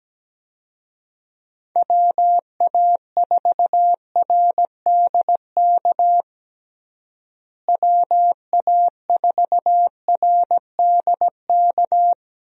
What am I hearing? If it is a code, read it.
Morse code, heard as a keyed tone, about 17 wpm, 700 Hz. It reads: WA4RDK WA4RDK
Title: USA Call Signs - 6 Characters - Repeated 3x - 17wpm